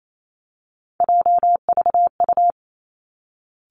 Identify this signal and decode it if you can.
Morse code, heard as a keyed tone, about 28 wpm, 700 Hz. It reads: JVU
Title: Three Challenging Letter-Number Combinations 28wpm